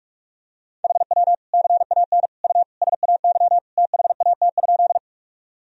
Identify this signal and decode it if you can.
Morse code, heard as a keyed tone, about 46 wpm, 700 Hz. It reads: HW CAN U SAY THAT?
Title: Sentences from Top 200 Words and Abbreviations 45wpm